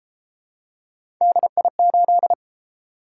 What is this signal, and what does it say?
Morse code, heard as a keyed tone, about 33 wpm, 700 Hz. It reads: DI8